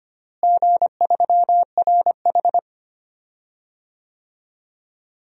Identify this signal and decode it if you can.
Morse code, heard as a keyed tone, about 25 wpm, 700 Hz. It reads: G3RH
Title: DXCC-100 Call Signs - 4 Characters - Repeated 3x - 25wpm